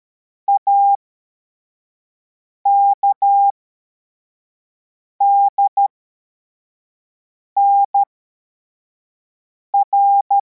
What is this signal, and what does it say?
Morse code, heard as a keyed tone, about 13 wpm, spaced as for 4 wpm, 800 Hz. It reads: AKDNR
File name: formation / cw / Text24